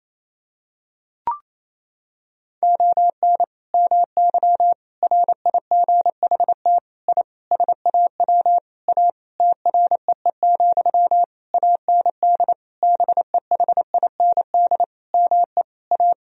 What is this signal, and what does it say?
Morse code, heard as a keyed tone, about 28 wpm, 700 Hz. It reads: ON MY RIGHT I SAW A TREE, AND BEHIND ME A
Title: Sentences from Top 1000 Words - Review Part 2 - 28wpm